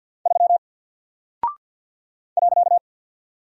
Morse code, heard as a keyed tone, about 48 wpm, 700 Hz. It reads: 3 .
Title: Single Character (Letter:Number:Punctuation:Prosign) 50wpm